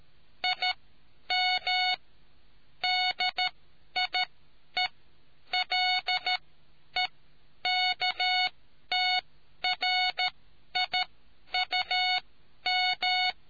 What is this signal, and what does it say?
Morse code, heard as a keyed tone, about 13 wpm, spaced as for 10 wpm, 740 Hz. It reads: IM DIELEKTRIUM